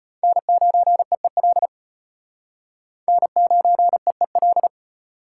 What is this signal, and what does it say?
Morse code, heard as a keyed tone, about 33 wpm, 700 Hz. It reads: N9EEL N9EEL